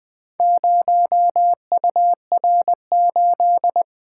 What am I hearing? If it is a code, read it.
Morse code, heard as a keyed tone, about 20 wpm, 700 Hz. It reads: 0UR8
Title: Four Letter-Number Combinations - 20wpm